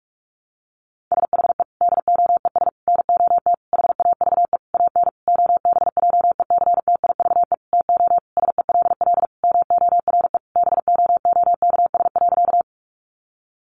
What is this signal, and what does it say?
Morse code, heard as a keyed tone, about 44 wpm, 700 Hz. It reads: SHE DOES NOT HAVE AN OBJECTIVE TO SELL MORE BOOKS.
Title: Sentences from Top 1200 Words 45wpm